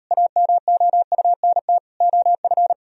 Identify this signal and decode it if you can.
Morse code, heard as a keyed tone, about 38 wpm, 700 Hz. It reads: AMOUNT OF